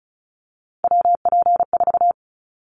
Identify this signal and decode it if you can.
Morse code, heard as a keyed tone, about 36 wpm, 700 Hz. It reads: WP4
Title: Callsign Prefixes for Top 25 DXCC - 35wpm